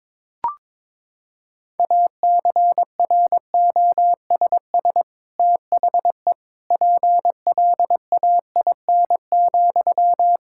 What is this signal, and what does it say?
Morse code, heard as a keyed tone, about 22 wpm, 700 Hz. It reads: ACROSS THE PLAIN,
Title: Sentences from Top 1000 Words - Review Part 5 - 22wpm